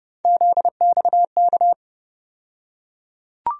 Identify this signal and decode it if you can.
Morse code, heard as a keyed tone, about 32 wpm, 700 Hz. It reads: ZXK
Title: USA Call Sign Postfixes - 3 Letters 30wpm